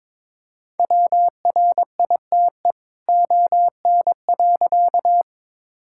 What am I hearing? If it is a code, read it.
Morse code, heard as a keyed tone, about 22 wpm, 700 Hz. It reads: WRITE ON.
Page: morse-code-ninja-practice-22wpm